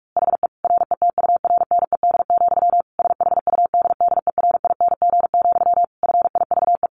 Morse code, heard as a keyed tone, about 44 wpm, 700 Hz. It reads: HE RETURNED, SHUDDERING, FIVE
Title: A Sound Of Thunder 45wpm